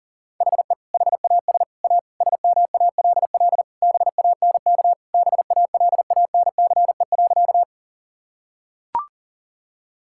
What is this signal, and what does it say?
Morse code, heard as a keyed tone, about 40 wpm, 700 Hz. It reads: HE HAS A SMALL BANK BALANCE.
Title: Sentences from Top 1000 Words - Review Part 6 - 40wpm